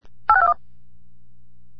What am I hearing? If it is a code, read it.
DTMF, heard as telephone touch tones, tones of about 70 ms. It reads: 8621